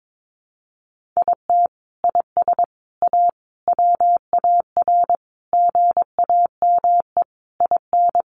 Morse code, heard as a keyed tone, about 22 wpm, 700 Hz. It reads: IT IS A WAR GAME IN
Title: Sentences from Top 1500 Words 22wpm